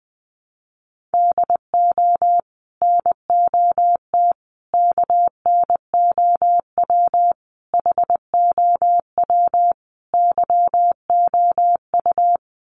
Morse code, heard as a keyed tone, about 19 wpm, 700 Hz. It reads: DO NOT KNOW HOW YOU
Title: Sets of 5 Words Encore 20wpm